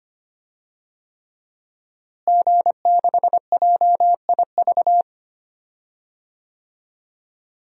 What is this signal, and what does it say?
Morse code, heard as a keyed tone, about 25 wpm, 700 Hz. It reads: G6JIV
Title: DXCC-100 Call Signs - 5 Characters - Repeated 3x - 25wpm